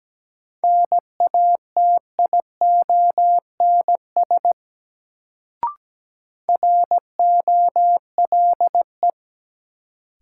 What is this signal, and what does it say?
Morse code, heard as a keyed tone, about 17 wpm, 700 Hz. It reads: NATIONS ROLE
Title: Sets of 3 Words - New Word Drill - 17wpm